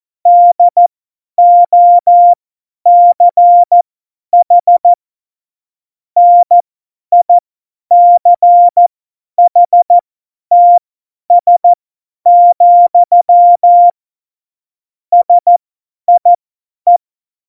Morse code, heard as a keyed tone, about 14 wpm, spaced as for 10 wpm, 700 Hz. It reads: DOCH NICHTS, SIE
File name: Morsekurs / Witze / Witz 09. April 2019, 50 F70